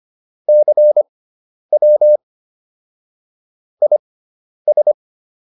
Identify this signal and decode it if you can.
Morse code, heard as a keyed tone, about 25 wpm, spaced as for 10 wpm, 600 Hz. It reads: CW IS
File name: QSO210 10